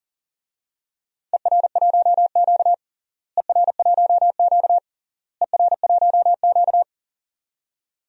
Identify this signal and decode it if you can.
Morse code, heard as a keyed tone, about 40 wpm, 700 Hz. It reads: ER1Q ER1Q ER1Q
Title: DXCC-100 Call Signs - 4 Characters - Repeated 3x - 40wpm